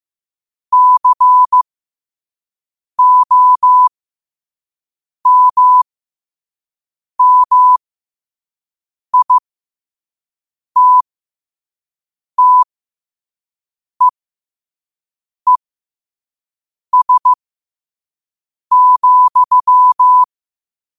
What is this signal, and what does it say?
Morse code, heard as a keyed tone, about 15 wpm, spaced as for 5 wpm, 1000 Hz. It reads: COMMITTEES,
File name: qotdfiles / 5wpm